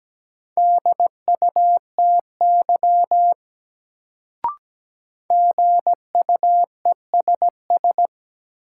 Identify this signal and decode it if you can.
Morse code, heard as a keyed tone, about 17 wpm, 700 Hz. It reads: DUTY GUESS
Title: Sentences from Top 1500 Words - New Word Drill Alt - 17wpm